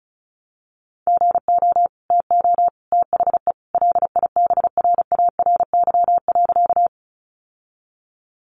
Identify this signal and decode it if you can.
Morse code, heard as a keyed tone, about 34 wpm, 700 Hz. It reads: GO TO THE LIBRARY.
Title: Sentences from Top 600 Words 35wpm